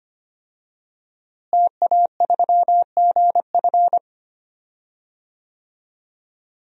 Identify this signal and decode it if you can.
Morse code, heard as a keyed tone, about 25 wpm, 700 Hz. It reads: TA3GF
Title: DXCC-100 Call Signs - 5 Characters - Repeated 3x - 25wpm